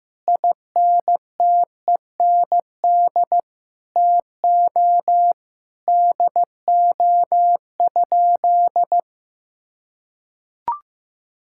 Morse code, heard as a keyed tone, about 16 wpm, 700 Hz. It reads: INTEND TO DO?
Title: Sentences from Top 1600 Words 15wpm